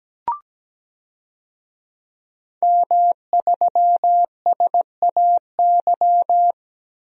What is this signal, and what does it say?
Morse code, heard as a keyed tone, about 17 wpm, 700 Hz. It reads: M3SAY
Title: DXCC-100 Call Signs - 5 Characters - Repeated 3x - 17wpm